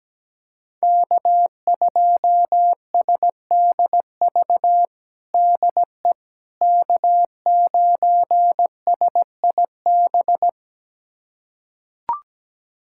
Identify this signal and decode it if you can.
Morse code, heard as a keyed tone, about 17 wpm, 700 Hz. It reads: K2SDV DE K9SIB
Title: USA Callsign DE Callsign - 5 Characters - 17wpm